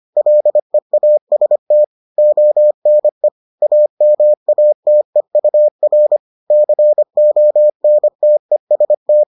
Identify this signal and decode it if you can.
Morse code, heard as a keyed tone, about 25 wpm, 600 Hz. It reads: LEAST ONE AMATEUR CONTEST